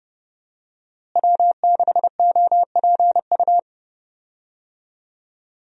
Morse code, heard as a keyed tone, about 29 wpm, 700 Hz. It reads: W6OPU
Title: USA Call Signs - 5 Characters - Repeated 2x - 30wpm